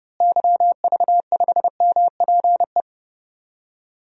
Morse code, heard as a keyed tone, about 30 wpm, 700 Hz. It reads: YV5MPE
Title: DXCC-100 Call Signs - 6 Characters - Repeated 3x - 30wpm